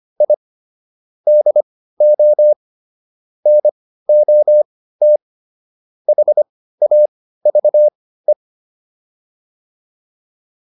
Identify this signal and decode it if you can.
Morse code, heard as a keyed tone, about 25 wpm, spaced as for 15 wpm, 600 Hz. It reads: I DO NOT HAVE